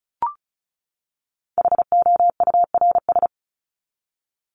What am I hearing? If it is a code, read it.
Morse code, heard as a keyed tone, about 34 wpm, 700 Hz. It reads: HOURS
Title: ES:S Suffix - Top 100 Words with Base Words 35wpm